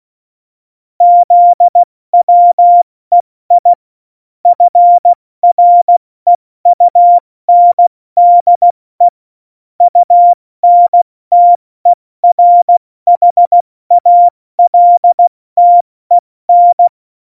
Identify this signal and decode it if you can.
Morse code, heard as a keyed tone, about 16 wpm, spaced as for 14 wpm, 700 Hz. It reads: ZWEI FREUNDE UNTERHALTEN
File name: Morsekurs / Witze / Witz 08. Januar 2019, 70 F80